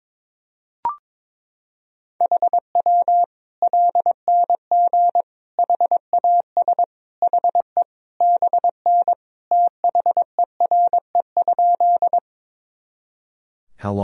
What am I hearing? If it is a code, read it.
Morse code, heard as a keyed tone, about 22 wpm, 700 Hz. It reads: HW LNG HAS HE BN THERE?
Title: Sentences from Top 100 Words and Abbreviations 22wpm